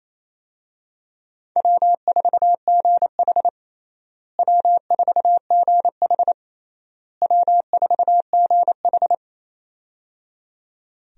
Morse code, heard as a keyed tone, about 28 wpm, 700 Hz. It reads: W4GH W4GH W4GH
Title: USA Call Signs - 4 Characters - Repeated 3x - 28wpm